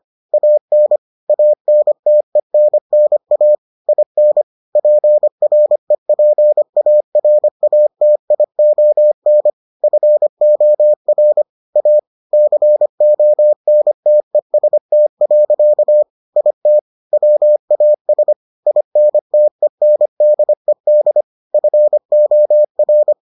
Morse code, heard as a keyed tone, about 25 wpm, 600 Hz. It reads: AN ANTENNA IN PREPARATION FOR A CONTEST. IT WAS INTENDED FOR